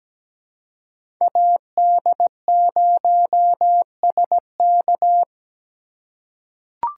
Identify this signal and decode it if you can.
Morse code, heard as a keyed tone, about 17 wpm, 700 Hz. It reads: AD0SK
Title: USA Call Signs - 5 Characters - 2x Character Spacing - 17wpm